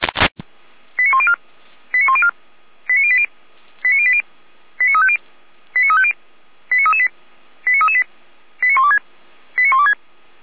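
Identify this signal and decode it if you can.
Five-tone selective call (ZVEI), heard as quick runs of five tones, tones of about 70 ms. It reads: 89104 89104 89080 89080 89350 89350 89338 89338 89126 89126
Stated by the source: ZVEI five-tone selective call received on an SDR.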